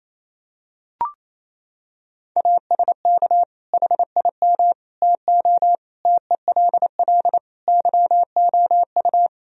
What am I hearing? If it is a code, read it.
Morse code, heard as a keyed tone, about 28 wpm, 700 Hz. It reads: ASK HIM TO TELL YOU
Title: Sentences from Top 1000 Words - Review Part 3 - 28wpm